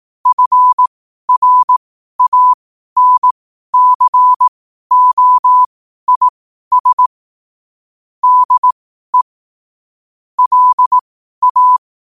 Morse code, heard as a keyed tone, about 18 wpm, spaced as for 12 wpm, 1000 Hz. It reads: FRANCOIS DE LA
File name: qotdfiles / 12wpm